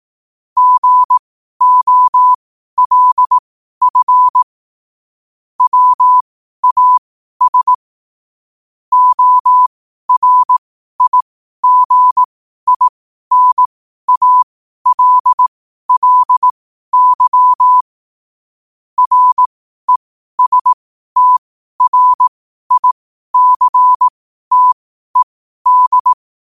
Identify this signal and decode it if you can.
Morse code, heard as a keyed tone, about 18 wpm, spaced as for 12 wpm, 1000 Hz. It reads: GOLF WAS ORIGINALLY RESTRICTED
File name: qotdfiles / 12wpm